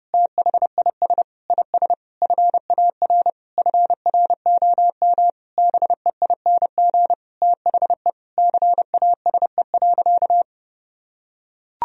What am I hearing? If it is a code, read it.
Morse code, heard as a keyed tone, about 30 wpm, 700 Hz. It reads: THIS IS FAR FROM BEING THE CASE.
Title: Sentences from Top 1000 Words - Review Part 3 - 30wpm